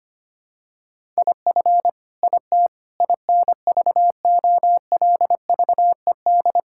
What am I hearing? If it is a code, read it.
Morse code, heard as a keyed tone, about 25 wpm, 700 Hz. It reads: IF IT INVOLVED
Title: Sentences from Top 1000 Words - Review Part 5 - 25wpm